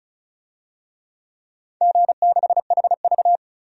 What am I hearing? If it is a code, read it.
Morse code, heard as a keyed tone, about 35 wpm, 700 Hz. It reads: G6HV
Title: DXCC-100 Call Signs - 4 Characters - 35wpm